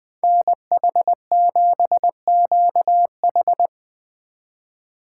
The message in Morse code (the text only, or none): NH7QH